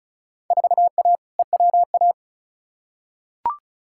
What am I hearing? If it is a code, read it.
Morse code, heard as a keyed tone, about 36 wpm, 700 Hz. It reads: 4A EWA
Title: Field Day Exchanges - 35wpm